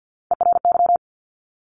Morse code, heard as a keyed tone, about 50 wpm, 700 Hz. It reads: ERY
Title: Top Three Letter Combinations in Words 50wpm